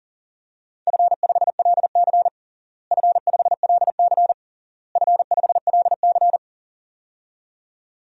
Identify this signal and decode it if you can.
Morse code, heard as a keyed tone, about 40 wpm, 700 Hz. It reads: F5LC F5LC F5LC
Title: DXCC-100 Call Signs - 4 Characters - Repeated 3x - 40wpm